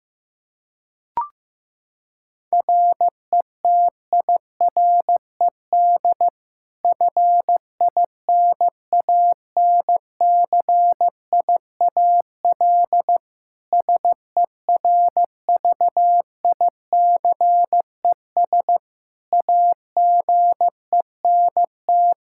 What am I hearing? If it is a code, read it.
Morse code, heard as a keyed tone, about 15 wpm, 700 Hz. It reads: RETIRED FINANCIAL SERVICES AGENT
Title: K1USN SST QSO Practice 15wpm